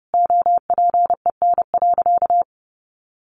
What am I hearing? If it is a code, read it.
Morse code, heard as a keyed tone, about 30 wpm, 700 Hz. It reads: OPEN.
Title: Sentences from Top 1400 Words 30wpm